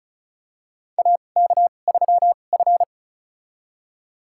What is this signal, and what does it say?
Morse code, heard as a keyed tone, about 36 wpm, spaced as for 25 wpm, 700 Hz. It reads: AK3F
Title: USA Call Signs - 4 Characters - 2x Character Spacing - 35wpm